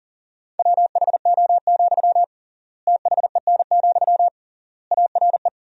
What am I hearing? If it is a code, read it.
Morse code, heard as a keyed tone, about 40 wpm, 700 Hz. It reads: WHO, THEN, ARE
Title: 3x Word-Spacing - Sentences from Top 100 Words - 40wpm